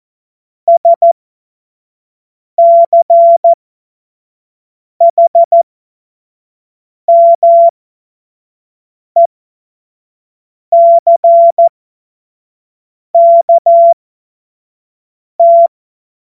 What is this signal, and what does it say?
Morse code, heard as a keyed tone, about 14 wpm, spaced as for 5 wpm, 680 Hz. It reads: SCHMECKT
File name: Morsekurs / Witze / Witz 19. Oktober 2021, 25 F70